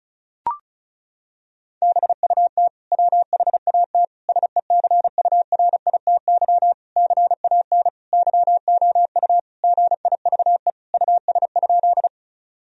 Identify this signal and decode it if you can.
Morse code, heard as a keyed tone, about 36 wpm, 700 Hz. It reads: BUT WHAT SECURITY CAN YOU GIVE US?
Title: Sentences from Top 1000 Words - Review Part 4 - 35wpm